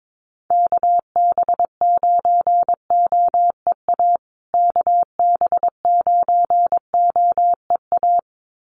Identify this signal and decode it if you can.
Morse code, heard as a keyed tone, about 22 wpm, 700 Hz. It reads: KB9OEA KB9OEA
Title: USA Call Signs Repeated 2x 22wpm